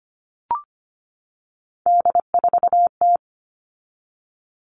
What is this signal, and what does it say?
Morse code, heard as a keyed tone, about 25 wpm, 700 Hz. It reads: D4T
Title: All Cut Numbers - 3-Digit 25wpm